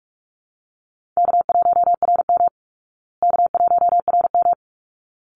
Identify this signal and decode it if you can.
Morse code, heard as a keyed tone, about 44 wpm, 700 Hz. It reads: K1RM K1RM
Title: Easy Call Signs Repeated 2x - 45wpm